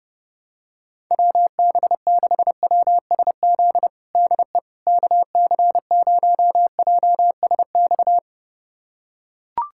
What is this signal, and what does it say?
Morse code, heard as a keyed tone, about 30 wpm, 700 Hz. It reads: WB6WSZ DE KC0JSX
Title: USA Callsign DE Callsign 30wpm